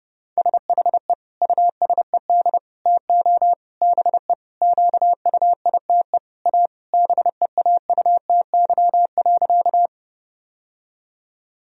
Morse code, heard as a keyed tone, about 30 wpm, 700 Hz. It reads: SHE USED TO BE QUITE A BEAUTY.
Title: Sentences from Top 1000 Words - Review Part 2 - 30wpm